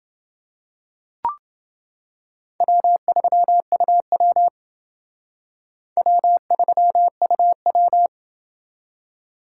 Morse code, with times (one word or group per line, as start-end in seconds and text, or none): 2.6-4.5 W3UW
6.0-8.1 W3UW